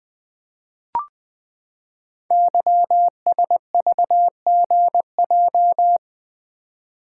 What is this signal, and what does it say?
Morse code, heard as a keyed tone, about 20 wpm, 700 Hz. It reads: YSVGJ
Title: Code Groups - 5 Letters - 20wpm